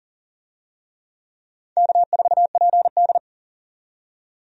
Morse code, heard as a keyed tone, about 40 wpm, 700 Hz. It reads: K4PD
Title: USA Call Signs - 4 Characters - 40wpm